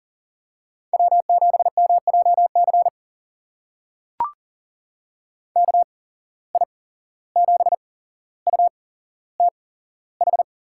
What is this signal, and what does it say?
Morse code, heard as a keyed tone, about 39 wpm, spaced as for 16 wpm, 700 Hz. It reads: W7MJC KI7UTH
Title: USA Call Signs - 8x Character Spacing - 40wpm